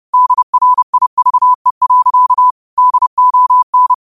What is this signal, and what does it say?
Morse code, heard as a keyed tone, about 30 wpm, 1000 Hz. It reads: DRIVE. DON
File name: qotdfiles / 30wpm